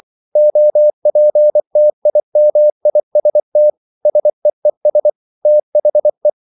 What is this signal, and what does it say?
Morse code, heard as a keyed tone, about 24 wpm, 600 Hz. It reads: OPTIMIST SEES THE
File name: CW / Practice / EnglishQuotations0424